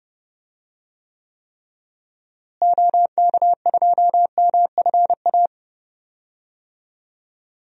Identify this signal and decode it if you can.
Morse code, heard as a keyed tone, about 31 wpm, 700 Hz. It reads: OK2MFA
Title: DXCC-100 Call Signs - 6 Characters - Repeated 3x - 30wpm